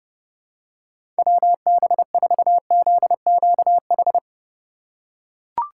Morse code, heard as a keyed tone, about 32 wpm, 700 Hz. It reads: WB4ZQH
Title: USA Call Signs - 6 Characters - 2x Character Spacing - 30wpm